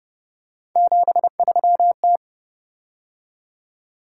Morse code, heard as a keyed tone, about 28 wpm, 700 Hz. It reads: 73T